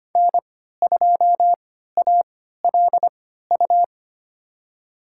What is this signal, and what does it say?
Morse code, heard as a keyed tone, about 25 wpm, spaced as for 14 wpm, 700 Hz. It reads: N2ALU
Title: USA Call Signs - 5 Characters - 3x Character Spacing - 25wpm